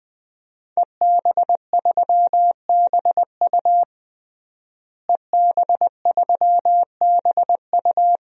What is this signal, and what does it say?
Morse code, heard as a keyed tone, about 20 wpm, 700 Hz. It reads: EB3BU EB3BU
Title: DXCC-100 Call Signs - 5 Characters - Repeated 2x - 20wpm